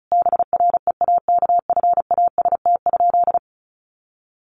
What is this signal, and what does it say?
Morse code, heard as a keyed tone, about 34 wpm, 700 Hz. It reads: BREAKFAST?
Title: Sentences from Top 1500 Words 35wpm